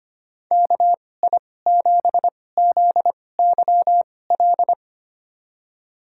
Morse code, heard as a keyed tone, about 25 wpm, spaced as for 18 wpm, 700 Hz. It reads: KI7ZYL
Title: USA Call Signs - 6 Characters - 2x Character Spacing - 25wpm